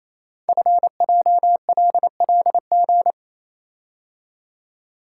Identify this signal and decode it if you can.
Morse code, heard as a keyed tone, about 27 wpm, 700 Hz. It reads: FJLLG
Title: Code Groups - 5 Letters - 28wpm